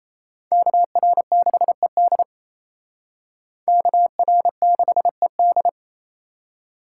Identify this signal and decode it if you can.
Morse code, heard as a keyed tone, about 35 wpm, 700 Hz. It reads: KR6ED KR6ED
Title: Speed Racing - USA 2x2 Call Signs 28wpm